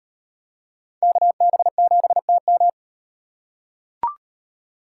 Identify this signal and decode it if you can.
Morse code, heard as a keyed tone, about 33 wpm, 700 Hz. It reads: KB7TM